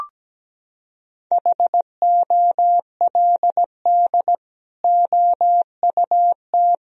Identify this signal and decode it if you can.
Morse code, heard as a keyed tone, about 17 wpm, 700 Hz. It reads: HOLD OUT
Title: Sentences from Top 1000 Words - Review Part 2 - 17wpm